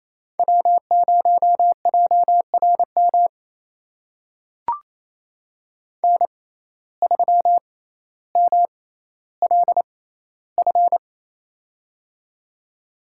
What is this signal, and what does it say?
Morse code, heard as a keyed tone, about 28 wpm, spaced as for 14 wpm, 700 Hz. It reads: W0JRM N3MLF
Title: USA Call Signs - 5 Characters - 6x Character Spacing - 28wpm